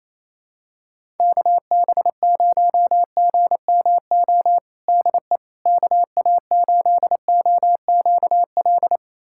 Morse code, heard as a keyed tone, about 28 wpm, 700 Hz. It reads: KB0GMO DE KA8OQL
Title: USA Callsign DE Callsign 28wpm